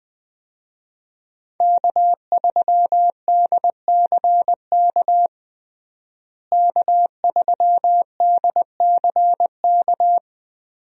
Morse code, heard as a keyed tone, about 20 wpm, 700 Hz. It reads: K3DCK K3DCK
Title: USA Call Signs - 5 Characters - Repeated 2x - 20wpm